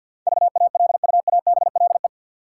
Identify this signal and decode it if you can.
Morse code, heard as a keyed tone, about 51 wpm, 700 Hz. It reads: VALUABLE
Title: Sentences from Top 1100 Words - New Word Drill - 50wpm